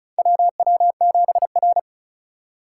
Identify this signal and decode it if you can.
Morse code, heard as a keyed tone, about 36 wpm, 700 Hz. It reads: WW7R
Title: USA Call Signs - 4 Characters - 4x Character Spacing - 35wpm